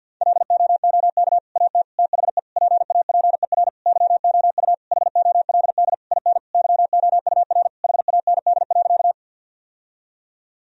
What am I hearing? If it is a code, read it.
Morse code, heard as a keyed tone, about 50 wpm, 700 Hz. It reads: LOOK AT THE PAPER YOU HOLD IN YOUR HAND.
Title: Sentences from Top 1000 Words - Review Part 2 - 50wpm